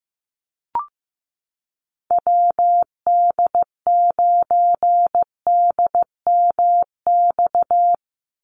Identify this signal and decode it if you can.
Morse code, heard as a keyed tone, about 15 wpm, 700 Hz. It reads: WD9DMX
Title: USA Call Signs 15wpm